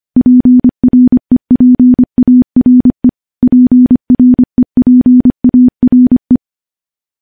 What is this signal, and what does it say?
Morse code, heard as a keyed tone, about 25 wpm, 250 Hz. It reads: PREPARE PREPARE
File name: morse / morse-training-25WPM